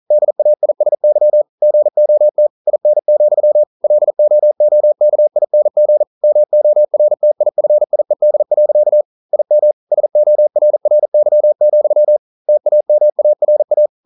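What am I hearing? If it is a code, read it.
Morse code, heard as a keyed tone, about 41 wpm, 600 Hz. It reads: DAISY GOT IN, LOOKING MORTIFIED. IM SORRY, TAMARA
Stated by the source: Morse code practice at 41 wpm, from CW med Gnister.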